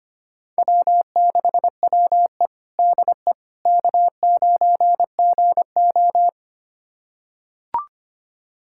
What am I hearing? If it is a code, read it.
Morse code, heard as a keyed tone, about 25 wpm, 700 Hz. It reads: W6WE DE K9GO
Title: USA Callsign DE Callsign - 4 Characters - 25wpm